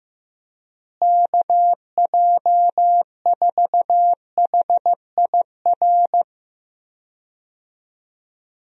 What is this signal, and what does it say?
Morse code, heard as a keyed tone, about 15 wpm, 700 Hz. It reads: KJ4HIR
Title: USA Call Signs - 6 Characters - Repeated 3x - 15wpm